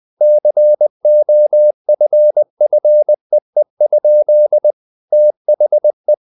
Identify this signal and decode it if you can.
Morse code, heard as a keyed tone, about 20 wpm, 600 Hz. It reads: COFFEE? THE
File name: the old man and the sea / the old man and the sea part 1 20wpm